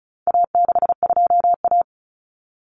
Morse code, heard as a keyed tone, about 34 wpm, 700 Hz. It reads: A62A
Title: Common Cut Numbers - 4-Digit 35wpm